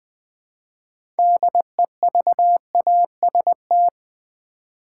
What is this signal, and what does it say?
Morse code, heard as a keyed tone, about 20 wpm, 700 Hz. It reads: DEVAST
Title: Build A Word - 3 Letters At A Time - 9-Letter Words - 20wpm